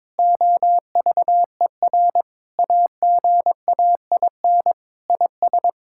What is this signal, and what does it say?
Morse code, heard as a keyed tone, about 22 wpm, 700 Hz. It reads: OVER AGAIN IS
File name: Sentences with 7-Word Phrases - 22wpm